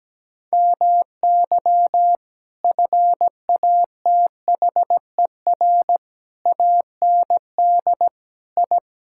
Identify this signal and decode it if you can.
Morse code, heard as a keyed tone, about 17 wpm, 700 Hz. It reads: MY FATHER AND I
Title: Sentences from Top 1000 Words - Review Part 6 - 17wpm